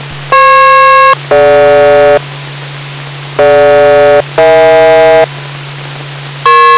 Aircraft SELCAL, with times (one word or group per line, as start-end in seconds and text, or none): 0.3-2.2 FP-DG
3.4-5.3 DG-EH